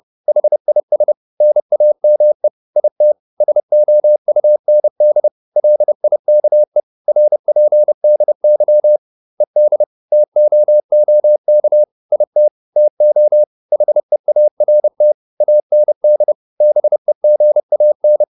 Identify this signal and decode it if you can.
Morse code, heard as a keyed tone, about 30 wpm, 600 Hz. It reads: HIS NAME IT SOUND LIKE RPDY ED TOOK IT TO HEART AND BEGAN